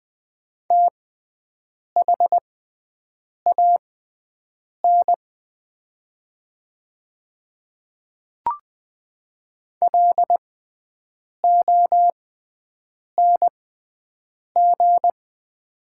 Morse code, heard as a keyed tone, about 20 wpm, spaced as for 7 wpm, 700 Hz. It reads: THAN LONG
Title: 6x Character Spacing - Top 100 Words - 20wpm